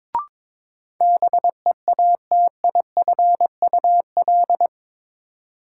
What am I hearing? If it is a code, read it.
Morse code, heard as a keyed tone, about 22 wpm, 700 Hz. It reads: BEATIFUL
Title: Sentences from Top 1300 Words - New Word Drill - 22wpm